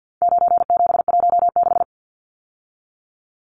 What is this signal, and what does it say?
Morse code, heard as a keyed tone, about 50 wpm, 700 Hz. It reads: 9716
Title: Numbers - 4-Digit 50wpm